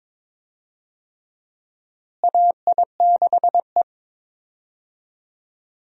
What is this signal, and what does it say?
Morse code, heard as a keyed tone, about 22 wpm, 700 Hz. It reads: AI6E
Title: USA Call Signs - 4 Characters - Repeated 3x - 22wpm